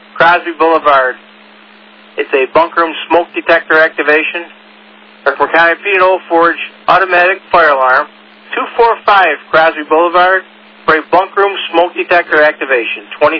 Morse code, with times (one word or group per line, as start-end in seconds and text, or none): none